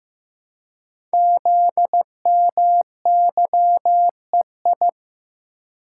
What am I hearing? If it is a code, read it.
Morse code, heard as a keyed tone, about 15 wpm, 690 Hz. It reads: ZMYEI
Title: Code Groups - 5 Letters - 15wpm